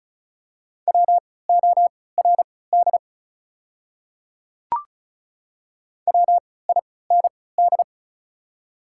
Morse code, heard as a keyed tone, about 36 wpm, spaced as for 20 wpm, 690 Hz. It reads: WORD WIND
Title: 3x Character Spacing - Top 500 Words - 35wpm